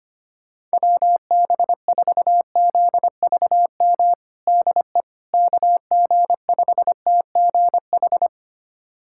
Morse code, heard as a keyed tone, about 25 wpm, 700 Hz. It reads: WB4ZVM DE KG5TGH
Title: USA Callsign DE Callsign - 6 Characters - 25wpm